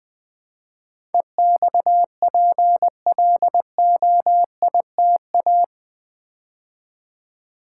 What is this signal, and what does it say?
Morse code, heard as a keyed tone, about 20 wpm, 700 Hz. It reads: EXPLOITA
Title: Build A Word - 4 Letters At A Time - 12-Letter Words - 20wpm